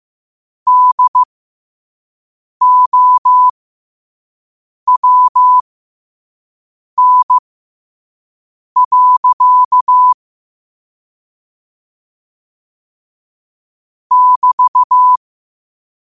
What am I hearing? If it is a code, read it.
Morse code, heard as a keyed tone, about 15 wpm, spaced as for 5 wpm, 1000 Hz. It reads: DOWN. <BT>
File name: qotdfiles / 5wpm